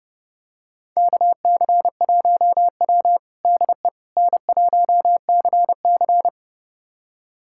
Morse code, heard as a keyed tone, about 30 wpm, 700 Hz. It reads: KC1W DE N1CC
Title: USA Callsign DE Callsign - 4 Characters - 30wpm